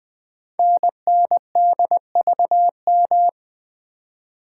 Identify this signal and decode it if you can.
Morse code, heard as a keyed tone, about 20 wpm, 700 Hz. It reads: NNDVM
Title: Code Groups - 5 Letters - 20wpm